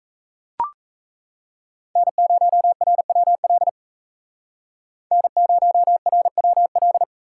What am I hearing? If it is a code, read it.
Morse code, heard as a keyed tone, about 37 wpm, 700 Hz. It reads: N0RWL N0RWL